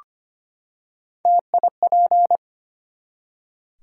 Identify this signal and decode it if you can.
Morse code, heard as a keyed tone, about 25 wpm, 700 Hz. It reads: TIP
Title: Three Letter Words - 25wpm